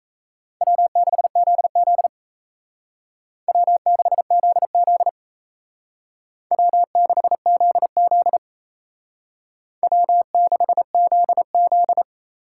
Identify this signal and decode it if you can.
Morse code, heard as a keyed tone, about 35 wpm, 700 Hz. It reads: W6ZZ W6ZZ W6ZZ W6ZZ
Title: Speed Racing - USA 1x2 Call Signs 28wpm